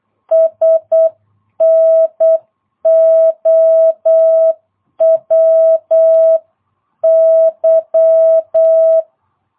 Morse code, heard as a keyed tone, about 8 wpm, 650 Hz. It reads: SNOWY
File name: jh8cbh / 2022cw